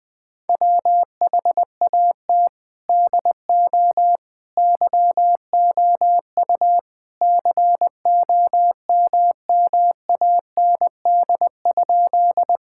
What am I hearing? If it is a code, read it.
Morse code, heard as a keyed tone, about 20 wpm, 700 Hz. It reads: WHAT DO YOU COMMAND?
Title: Sentences from Top 1600 Words 20wpm